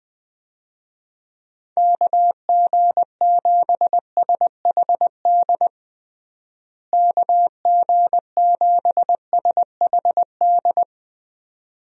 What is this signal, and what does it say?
Morse code, heard as a keyed tone, about 20 wpm, 700 Hz. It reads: KG7SHD KG7SHD
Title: USA Call Signs - 6 Characters - Repeated 2x - 20wpm